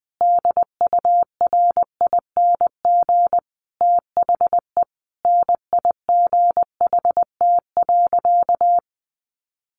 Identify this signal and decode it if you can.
Morse code, heard as a keyed tone, about 20 wpm, 700 Hz. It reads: DURING THE NIGHT.
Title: Sentences from Top 1400 Words 20wpm